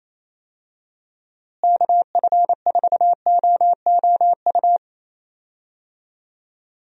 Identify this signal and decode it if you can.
Morse code, heard as a keyed tone, about 28 wpm, 700 Hz. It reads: KF4OOU
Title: USA Call Signs - 6 Characters - Repeated 2x - 28wpm